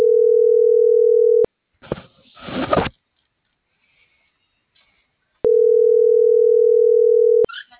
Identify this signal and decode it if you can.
Ringback tone (North America), 440 + 480 Hz.